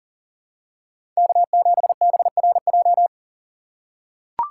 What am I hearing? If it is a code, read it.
Morse code, heard as a keyed tone, about 40 wpm, 700 Hz. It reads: K7BRJ
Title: USA Call Signs - 5 Characters - 6x Character Spacing - 40wpm